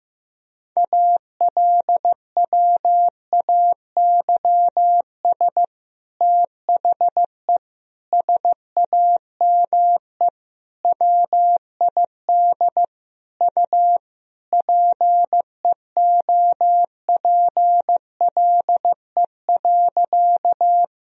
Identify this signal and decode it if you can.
Morse code, heard as a keyed tone, about 15 wpm, 700 Hz. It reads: ALWAYS THE SAME WID U PEOPLE.
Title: Sentences from Top 300 Words and Abbreviations 15wpm